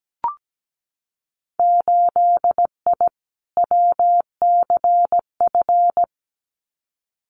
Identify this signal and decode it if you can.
Morse code, heard as a keyed tone, about 17 wpm, 700 Hz. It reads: 8I WCF
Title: Winter Field Day Exchanges Encore 17wpm